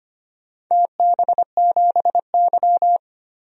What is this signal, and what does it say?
Morse code, heard as a keyed tone, about 25 wpm, 700 Hz. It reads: TB7Y
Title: DXCC-100 Call Signs - 4 Characters - 25wpm